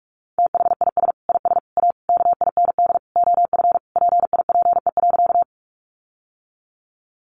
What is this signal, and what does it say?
Morse code, heard as a keyed tone, about 46 wpm, 700 Hz. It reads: THIS IS A KIND OF PIPE.
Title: Sentences from Top 1500 Words 45wpm